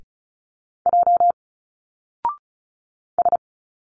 Morse code, heard as a keyed tone, about 35 wpm, 700 Hz. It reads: J S
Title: Single Letters 35wpm